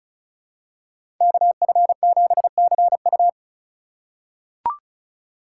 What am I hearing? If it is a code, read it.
Morse code, heard as a keyed tone, about 34 wpm, 700 Hz. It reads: KF7CU
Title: USA Call Signs - 5 Characters - 3x Character Spacing - 35wpm